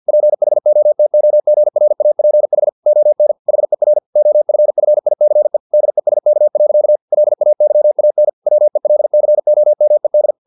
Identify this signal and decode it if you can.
Morse code, heard as a keyed tone, about 50 wpm, 600 Hz. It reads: PHOTOGRAPH ON HER OFFICE DESK. LAYAN WELCOMED
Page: Never 49wpm